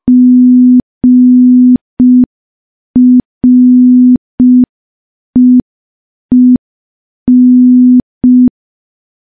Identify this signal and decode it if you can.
Morse code, heard as a keyed tone, about 5 wpm, 250 Hz. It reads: GREEN
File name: morse / morse-training-05WPM